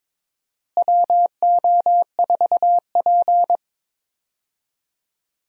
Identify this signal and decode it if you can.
Morse code, heard as a keyed tone, about 22 wpm, 700 Hz. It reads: WO4P
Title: USA Call Signs - 4 Characters - Repeated 3x - 22wpm